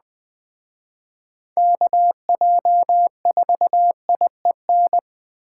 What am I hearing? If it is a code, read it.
Morse code, heard as a keyed tone, about 19 wpm, 700 Hz. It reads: KJ4IEN